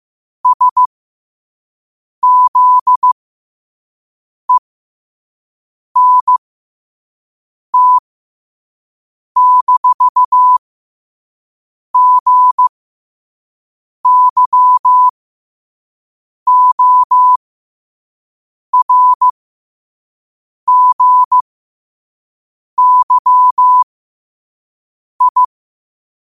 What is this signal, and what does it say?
Morse code, heard as a keyed tone, about 15 wpm, spaced as for 5 wpm, 1000 Hz. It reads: SZENT-GYORGYI
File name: qotdfiles / 5wpm